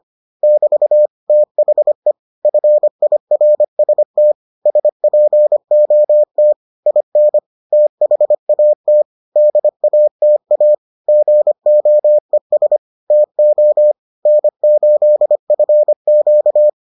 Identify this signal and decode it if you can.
Morse code, heard as a keyed tone, about 25 wpm, 600 Hz. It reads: <BT> THE FIRST SPOT IN THAT DATA GOES TO N8FQ